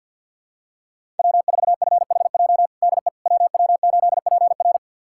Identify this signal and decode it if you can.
Morse code, heard as a keyed tone, about 51 wpm, 700 Hz. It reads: W4FHJ DE WW8PR
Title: USA Callsign DE Callsign - 5 Characters - 50wpm